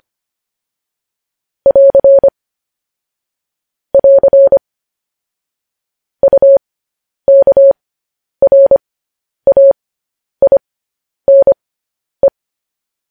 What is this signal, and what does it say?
Morse code, heard as a keyed tone, about 25 wpm, spaced as for 10 wpm, 550 Hz. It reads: <AR> <AR> UKRAINE
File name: podcast / News headlines-1x-25x10